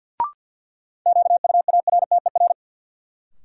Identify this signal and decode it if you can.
Morse code, heard as a keyed tone, about 48 wpm, 700 Hz. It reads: QUARTER
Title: Sentences from Top 1300 Words - New Word Drill Alt - 50wpm